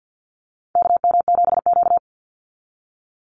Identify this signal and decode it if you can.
Morse code, heard as a keyed tone, about 50 wpm, 700 Hz. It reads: KM7Q
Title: Call Signs - 50wpm